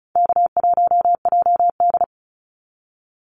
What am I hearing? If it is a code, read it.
Morse code, heard as a keyed tone, about 34 wpm, 700 Hz. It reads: K1JD
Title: NA SOTA Activator Callsigns 35wpm